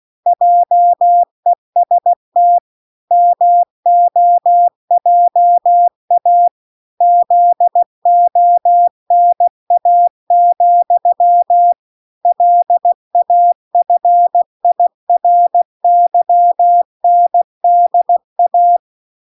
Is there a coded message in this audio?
Yes: Morse code from 0.3 to 18.8 s, heard as a keyed tone, about 16 wpm, 700 Hz. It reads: JEST MOJA ZONA, LAFIRYNDA